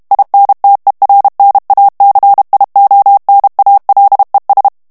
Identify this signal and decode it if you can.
Morse code, heard as a keyed tone, about 32 wpm, 800 Hz. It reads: INTERNACIONALES
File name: ESPAGNOL TEXTE EN CLAIR 2 VITESSE 30 MOTS MINUTES